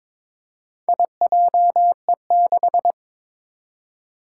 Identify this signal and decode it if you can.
Morse code, heard as a keyed tone, about 22 wpm, 700 Hz. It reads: IJE6
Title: Four Challenging Letter-Number Combinations 22wpm